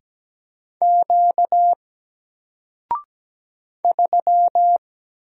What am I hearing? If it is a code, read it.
Morse code, heard as a keyed tone, about 17 wpm, 700 Hz. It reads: Q 3
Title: Single Letter-Number 17wpm